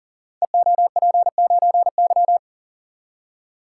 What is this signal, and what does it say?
Morse code, heard as a keyed tone, about 42 wpm, 700 Hz. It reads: EOP9Y
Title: Code Groups - 5 Letter:Numbers - 40wpm